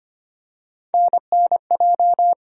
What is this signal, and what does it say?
Morse code, heard as a keyed tone, about 25 wpm, 700 Hz. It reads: NNJ
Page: morse-code-ninja-practice-25wpm